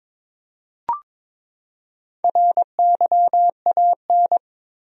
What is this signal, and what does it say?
Morse code, heard as a keyed tone, about 22 wpm, 700 Hz. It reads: RYAN
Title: USA Common Names - Max 5 Letters - 22wpm